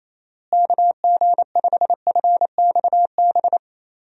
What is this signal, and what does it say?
Morse code, heard as a keyed tone, about 27 wpm, 700 Hz. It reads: KG5FXB
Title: USA Call Signs - 6 Characters - Repeated 3x - 28wpm